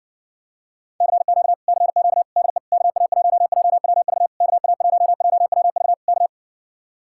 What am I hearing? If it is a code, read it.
Morse code, heard as a keyed tone, about 70 wpm, 700 Hz. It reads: CQ CQ DE KA1JWV KA1JWV K